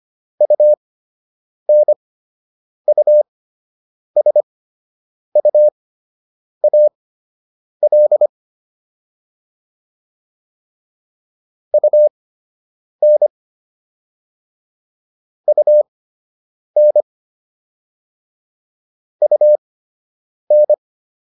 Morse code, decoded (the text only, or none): UNUSUAL UN UN UN